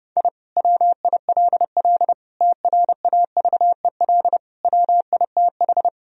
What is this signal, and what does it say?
Morse code, heard as a keyed tone, about 32 wpm, 700 Hz. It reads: I WILL TRAVEL WITH
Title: Sentences from Top 1000 Words - Review Part 2 - 30wpm